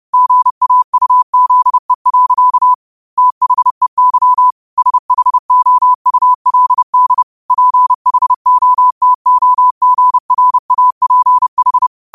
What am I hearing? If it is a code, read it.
Morse code, heard as a keyed tone, about 30 wpm, 1000 Hz. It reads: GAUZE. THEY SHOULD PHOTOGRAPH
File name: qotdfiles / 30wpm